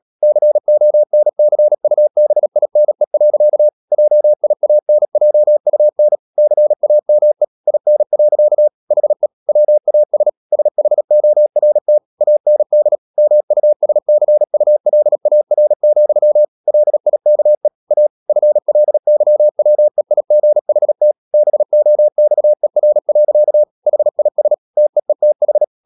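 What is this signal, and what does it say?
Morse code, heard as a keyed tone, about 37 wpm, 600 Hz. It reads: CONCUBINE. JIANJUN CAME IN. HE WAS SHORT AND MUSCULAR, LIKE A FLYWEIGHT BOXER. HIS TEETH
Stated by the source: Morse code practice at 35–39 wpm sending English text of Ken Follett's novel Never, from CW med Gnister.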